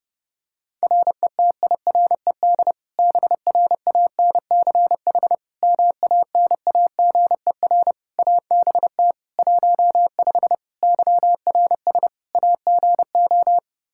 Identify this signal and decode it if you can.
Morse code, heard as a keyed tone, about 30 wpm, 700 Hz. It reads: RETIRED BRANCH MANAGER ABT 15 YRS AGO
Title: Practice QSO - Exchange 3 - More Info - 30wpm